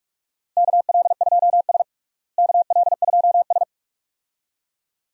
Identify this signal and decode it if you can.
Morse code, heard as a keyed tone, about 42 wpm, 700 Hz. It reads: KL2S KL2S
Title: USA Call Signs - 4 Characters - Repeated 2x - 45wpm